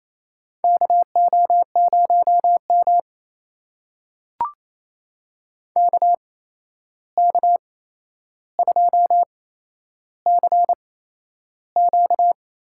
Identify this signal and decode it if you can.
Morse code, heard as a keyed tone, about 28 wpm, spaced as for 9 wpm, 700 Hz. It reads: KO0MK K2CQ
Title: USA Call Signs - 8x Character Spacing - 28wpm